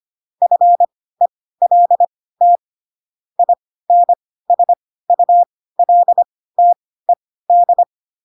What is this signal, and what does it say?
Morse code, heard as a keyed tone, about 25 wpm, spaced as for 16 wpm, 700 Hz. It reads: FELT INSULTED
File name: Pinocchio / Pinocchio ch3 10 25 ik1hge